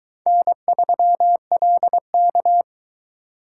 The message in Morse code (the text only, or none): N3LK